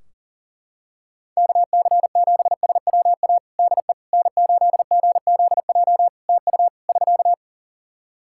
Morse code, heard as a keyed tone, about 40 wpm, 700 Hz. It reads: KC7SWA DE N8GZJ TU <SK>
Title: Practice QSO - Exchange 4 - Ending - 40wpm